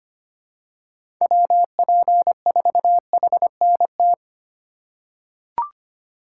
Morse code, heard as a keyed tone, about 25 wpm, 700 Hz. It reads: WP4HNT
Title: USA Call Signs - 6 Characters - 4x Character Spacing - 25wpm